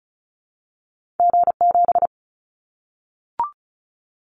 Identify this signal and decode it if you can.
Morse code, heard as a keyed tone, about 34 wpm, 700 Hz. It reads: G7
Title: Callsign Prefixes for Top 25 DXCC - 35wpm